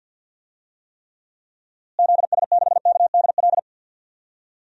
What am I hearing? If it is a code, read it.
Morse code, heard as a keyed tone, about 52 wpm, 700 Hz. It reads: ZS6KDL
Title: DXCC-100 Call Signs - 6 Characters - Repeated 2x - 50wpm